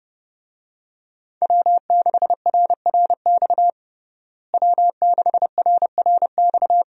Morse code, heard as a keyed tone, about 32 wpm, 700 Hz. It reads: W6RRX W6RRX
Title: USA Call Signs - 5 Characters - Repeated 2x - 30wpm